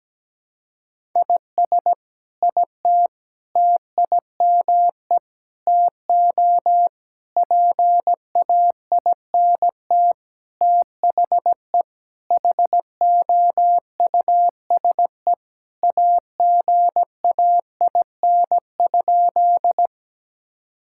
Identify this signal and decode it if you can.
Morse code, heard as a keyed tone, about 17 wpm, 700 Hz. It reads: IS IT TIME TO PAINT THE HOUSE AGAIN?
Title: Sentences from Top 1000 Words - Review Part 6 - 17wpm